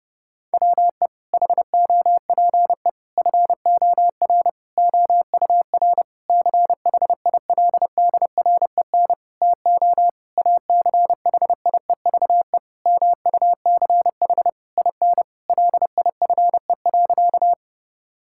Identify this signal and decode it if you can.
Morse code, heard as a keyed tone, about 30 wpm, 700 Hz. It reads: WE HOPE FOR OUR CHILDREN TO ACHIEVE MUCH IN LIFE.
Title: Sentences from Top 1000 Words - Review Part 6 - 30wpm